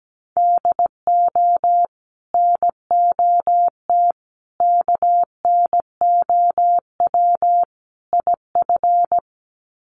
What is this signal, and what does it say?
Morse code, heard as a keyed tone, about 17 wpm, 700 Hz. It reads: DO NOT KNOW IF
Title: Sets of 4 Words 17wpm